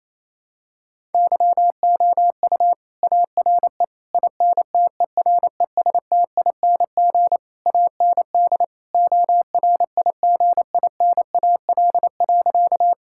Morse code, heard as a keyed tone, about 28 wpm, 700 Hz. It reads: YOU ARE INTERESTING AND ORIGINAL.
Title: Sentences from Top 1000 Words - Review Part 6 - 28wpm